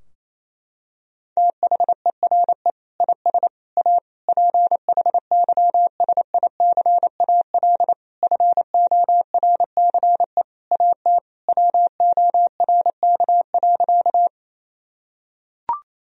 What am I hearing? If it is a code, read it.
Morse code, heard as a keyed tone, about 28 wpm, 700 Hz. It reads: THERE IS A PHYSICAL FORCE AT WORK.
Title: Sentences from Top 1000 Words - Review Part 4 - 28wpm